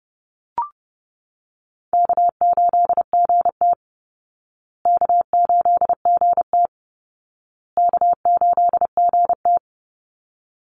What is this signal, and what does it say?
Morse code, heard as a keyed tone, about 30 wpm, 700 Hz. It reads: K8GT K8GT K8GT
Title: Call Signs Repeated 2x - 30wpm